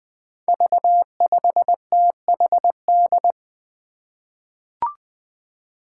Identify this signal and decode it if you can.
Morse code, heard as a keyed tone, about 20 wpm, 700 Hz. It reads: V5THD
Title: Five Challenging Letter-Number Combinations 20wpm